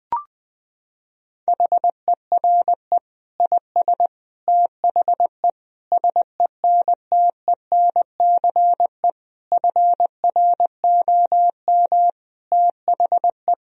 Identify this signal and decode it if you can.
Morse code, heard as a keyed tone, about 20 wpm, 700 Hz. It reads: HERE IS THE SENTENCE FROM THE
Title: Sentences from Top 1000 Words - Review Part 1 - 20wpm